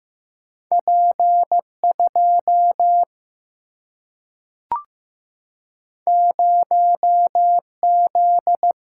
Morse code, heard as a keyed tone, about 15 wpm, 700 Hz. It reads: P 2 0 Z
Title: Single-Letter Single-Number 15wpm